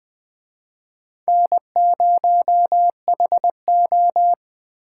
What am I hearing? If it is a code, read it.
Morse code, heard as a keyed tone, about 23 wpm, 700 Hz. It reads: N0HO
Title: Speed Racing - USA 1x2 Call Signs 20wpm